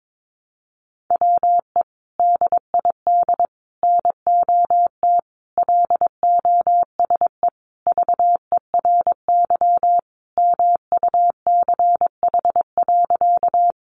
Morse code, heard as a keyed tone, about 22 wpm, 700 Hz. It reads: WE DID NOT LOSE VERY MUCH.
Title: Sentences from Top 1100 Words 22wpm